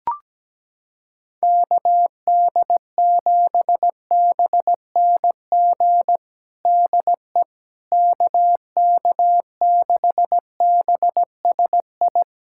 Morse code, heard as a keyed tone, about 17 wpm, 700 Hz. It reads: KD7BNG DE KK6BSI
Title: USA Callsign DE Callsign - 6 Characters - 17wpm